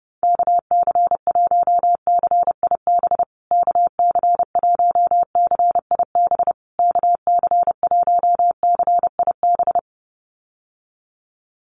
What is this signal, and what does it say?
Morse code, heard as a keyed tone, about 28 wpm, 700 Hz. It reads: KC1CIB KC1CIB KC1CIB
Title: USA Call Signs Repeated 3x 30wpm